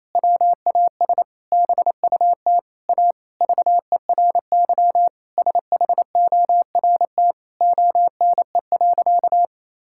Morse code, heard as a keyed tone, about 28 wpm, 700 Hz. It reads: WAS BUT A VERY SHORT ONE.
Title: Sentences from Top 1600 Words 28wpm